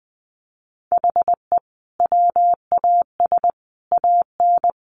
Morse code, heard as a keyed tone, about 20 wpm, 700 Hz. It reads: HE WAS AN
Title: Sentences from Top 1200 Words 20wpm